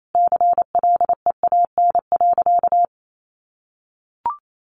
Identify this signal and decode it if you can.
Morse code, heard as a keyed tone, about 28 wpm, 700 Hz. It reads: CLEAN.
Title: Sentences from Top 1100 Words 28wpm